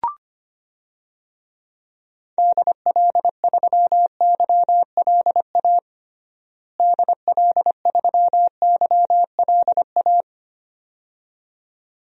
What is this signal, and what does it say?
Morse code, heard as a keyed tone, about 25 wpm, 700 Hz. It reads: DL3YLA DL3YLA
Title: DXCC-100 Call Signs - 6 Characters - Repeated 2x - 25wpm